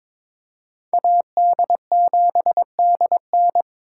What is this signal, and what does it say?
Morse code, heard as a keyed tone, about 26 wpm, 700 Hz. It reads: AD7DN
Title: Speed Racing - USA 2x2 Call Signs 22wpm